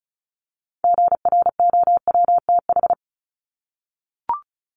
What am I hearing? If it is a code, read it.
Morse code, heard as a keyed tone, about 34 wpm, 700 Hz. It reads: GROWTH
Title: Top 751-1000 Words 35wpm